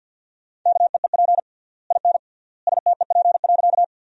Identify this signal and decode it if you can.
Morse code, heard as a keyed tone, about 51 wpm, 700 Hz. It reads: KEEP IN STEP.
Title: 3x Word-Spacing - Sentences from Top 400 Words - 50wpm